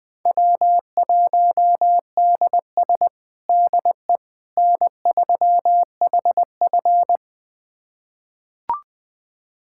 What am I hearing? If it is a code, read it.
Morse code, heard as a keyed tone, about 21 wpm, 700 Hz. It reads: W1DS DE N3HF
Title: USA Callsign DE Callsign - 4 Characters - 20wpm